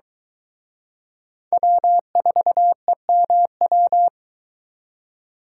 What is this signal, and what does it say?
Morse code, heard as a keyed tone, about 23 wpm, 700 Hz. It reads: W4EMW